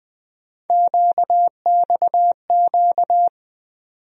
Morse code, heard as a keyed tone, about 21 wpm, 700 Hz. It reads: QXQ
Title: USA Call Sign Postfixes - 3 Letters 20wpm